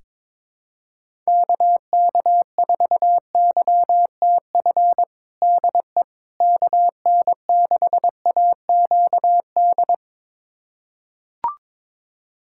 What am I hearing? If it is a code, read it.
Morse code, heard as a keyed tone, about 22 wpm, 700 Hz. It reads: KK4YTF DE KN6AQD
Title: USA Callsign DE Callsign - 6 Characters - 22wpm